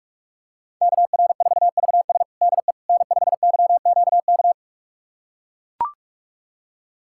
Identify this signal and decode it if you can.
Morse code, heard as a keyed tone, about 42 wpm, 700 Hz. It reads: KR4VS DE N5YQK
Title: USA Callsign DE Callsign - 5 Characters - 45wpm